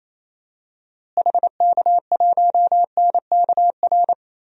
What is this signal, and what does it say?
Morse code, heard as a keyed tone, about 28 wpm, 700 Hz. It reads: HK1NKR
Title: DXCC-100 Call Signs - 6 Characters - Repeated 2x - 28wpm